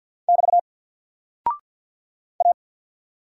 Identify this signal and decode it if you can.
Morse code, heard as a keyed tone, about 50 wpm, 700 Hz. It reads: <BT> A
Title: Single Character (Letter:Number:Punctuation:Prosign) 50wpm